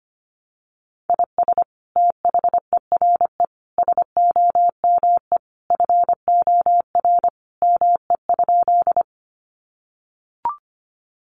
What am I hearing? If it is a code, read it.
Morse code, heard as a keyed tone, about 25 wpm, 700 Hz. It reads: IS THERE SOME FOR ME?
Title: Sentences from Top 200 Words 25wpm